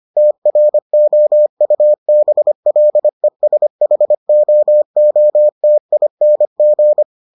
Morse code, heard as a keyed tone, about 25 wpm, 600 Hz. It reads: TROUBLESHOOTING